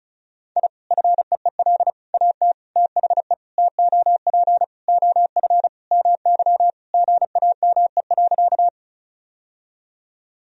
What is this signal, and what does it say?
Morse code, heard as a keyed tone, about 36 wpm, 700 Hz. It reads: I FEEL AT THE TOP OF MY GAME.
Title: Sentences from Top 1000 Words - Review Part 2 - 35wpm